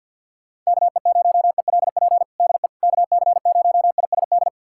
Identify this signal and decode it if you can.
Morse code, heard as a keyed tone, about 51 wpm, 700 Hz. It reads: KE0ELP DE KC0ISD
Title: USA Callsign DE Callsign - 6 Characters - 50wpm